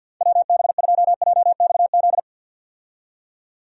Morse code, heard as a keyed tone, about 52 wpm, 700 Hz. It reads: WB2JXZ
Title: USA Call Signs - 6 Characters - 50wpm